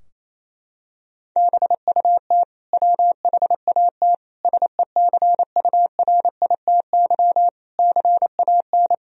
Morse code, heard as a keyed tone, about 28 wpm, 700 Hz. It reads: BUT WHAT SECURITY CAN
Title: Sentences from Top 1000 Words - Review Part 4 - 28wpm